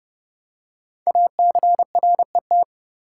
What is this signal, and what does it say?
Morse code, heard as a keyed tone, about 30 wpm, 690 Hz. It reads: ACRET
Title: Code Groups - 5 Letters - 30wpm